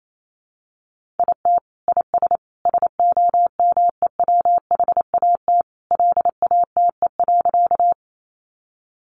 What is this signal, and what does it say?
Morse code, heard as a keyed tone, about 28 wpm, 700 Hz. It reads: IT IS SOMEWHAT LATE.
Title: Sentences from Top 1300 Words 28wpm